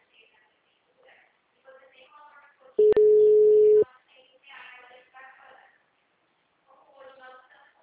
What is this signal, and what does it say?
Ringback tone (Europe, 425 Hz), with speech.
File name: out- -2009- - - .27886